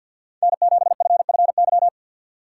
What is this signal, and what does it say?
Morse code, heard as a keyed tone, about 53 wpm, 700 Hz. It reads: N7FFY
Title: USA Call Signs - 5 Characters - 6x Character Spacing - 50wpm